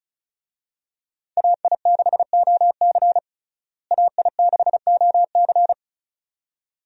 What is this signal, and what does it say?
Morse code, heard as a keyed tone, about 36 wpm, 700 Hz. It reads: AI6OC AI6OC
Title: USA Call Signs - 5 Characters - Repeated 2x - 35wpm